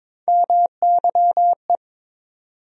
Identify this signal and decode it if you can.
Morse code, heard as a keyed tone, about 22 wpm, 700 Hz. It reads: MYE
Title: USA Call Sign Postfixes - 3 Letters 22wpm